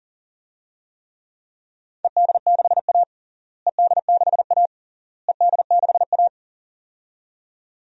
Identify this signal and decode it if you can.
Morse code, heard as a keyed tone, about 40 wpm, 700 Hz. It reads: ED6A ED6A ED6A
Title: DXCC-100 Call Signs - 4 Characters - Repeated 3x - 40wpm